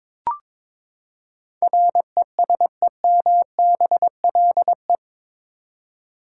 Morse code, heard as a keyed tone, about 22 wpm, 700 Hz. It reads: RESEMBLE
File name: practice / 22wpm / Eight Letter Words - 22wpm